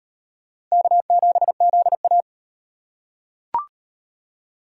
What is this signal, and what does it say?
Morse code, heard as a keyed tone, about 38 wpm, 700 Hz. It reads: K7ZA